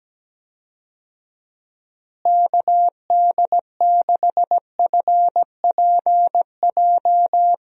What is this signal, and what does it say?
Morse code, heard as a keyed tone, about 17 wpm, 700 Hz. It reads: KD6FPJ
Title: USA Call Signs - 6 Characters - Repeated 3x - 17wpm